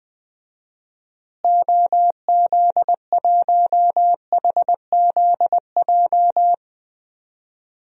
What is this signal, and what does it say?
Morse code, heard as a keyed tone, about 21 wpm, 700 Hz. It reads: OZ1HZJ
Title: DXCC-100 Call Signs - 6 Characters - Repeated 3x - 20wpm